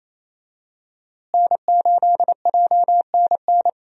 Morse code, heard as a keyed tone, about 28 wpm, 700 Hz. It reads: N8JNN
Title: USA Call Signs - 5 Characters - Repeated 3x - 28wpm